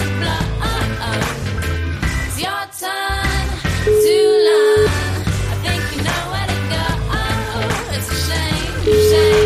Ringback tone (Europe, 425 Hz), with noise.